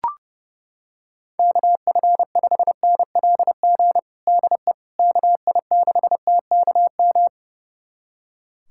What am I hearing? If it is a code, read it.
Morse code, heard as a keyed tone, about 30 wpm, 700 Hz. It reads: KF5NLG DE KI6TKM
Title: USA Callsign DE Callsign - 6 Characters - 30wpm